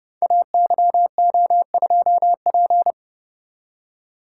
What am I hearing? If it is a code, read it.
Morse code, heard as a keyed tone, about 30 wpm, 700 Hz. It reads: AYO2P
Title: Code Groups - 5 Letter:Numbers - 30wpm